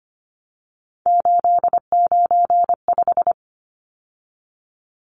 Morse code, heard as a keyed tone, about 25 wpm, 700 Hz. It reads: 895
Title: Numbers - 3-Digit 25wpm